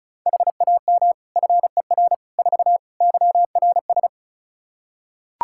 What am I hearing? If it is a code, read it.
Morse code, heard as a keyed tone, about 36 wpm, 700 Hz. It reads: HAM FER 4 YRS
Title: Practice QSO - Exchange 3 - More Info - 35wpm